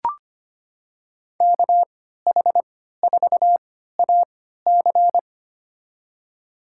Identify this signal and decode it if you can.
Morse code, heard as a keyed tone, about 25 wpm, spaced as for 14 wpm, 700 Hz. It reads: KH4AC
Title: USA Call Signs - 3x Character Spacing - 25wpm